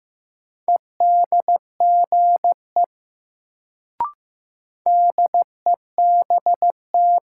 Morse code, heard as a keyed tone, about 15 wpm, 700 Hz. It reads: EDGE DEBT
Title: Sentences from Top 1300 Words - New Word Drill - 15wpm